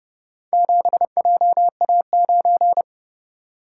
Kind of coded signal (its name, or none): Morse code